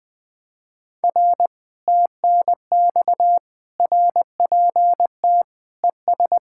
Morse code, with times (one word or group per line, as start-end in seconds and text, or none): 1.0-1.5 R
1.9-3.4 TNX
3.8-5.4 RPT
5.8-6.4 ES